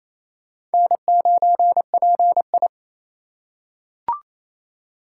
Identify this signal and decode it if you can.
Morse code, heard as a keyed tone, about 27 wpm, 700 Hz. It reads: N9PI
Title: USA Call Signs - 4 Characters - 4x Character Spacing - 28wpm